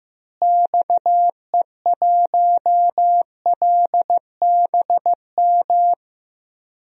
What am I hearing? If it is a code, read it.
Morse code, heard as a keyed tone, about 15 wpm, 700 Hz. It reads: XE1LBM
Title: DXCC-100 Call Signs - 6 Characters - Repeated 2x - 15wpm